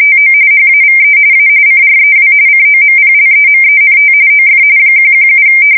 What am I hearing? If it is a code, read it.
RTTY, about 45.45 baud. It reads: TO HAVE A CREW COMPLEMENT OF 117
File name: fsk170 / USS-New-Jersey